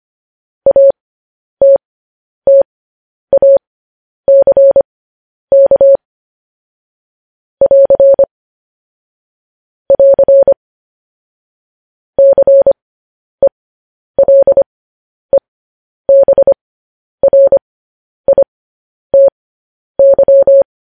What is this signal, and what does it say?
Morse code, heard as a keyed tone, about 25 wpm, spaced as for 10 wpm, 550 Hz. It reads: ATTACK <AR> <AR> CELEBRITY
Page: News Headlines in Morse Code at 10 WPM